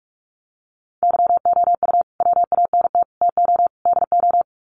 Morse code, heard as a keyed tone, about 44 wpm, 700 Hz. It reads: YOU WANT TO DO
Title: Sets of 4 Words 45wpm